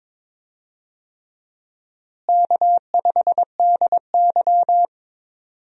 Morse code, heard as a keyed tone, about 22 wpm, 700 Hz. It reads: K5DY
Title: USA Call Signs - 4 Characters - Repeated 3x - 22wpm